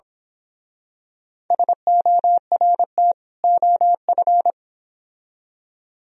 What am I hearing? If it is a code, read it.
Morse code, heard as a keyed tone, about 27 wpm, 700 Hz. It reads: SORT OF
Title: Speed-Racing - Sets of 2 Words 22wpm